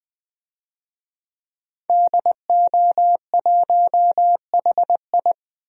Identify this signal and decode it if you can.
Morse code, heard as a keyed tone, about 20 wpm, 700 Hz. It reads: DO1HI
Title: DXCC-100 Call Signs - 5 Characters - Repeated 2x - 20wpm